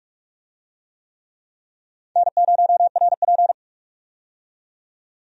Morse code, heard as a keyed tone, about 46 wpm, 700 Hz. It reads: N0RP
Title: USA Call Signs - 4 Characters - 45wpm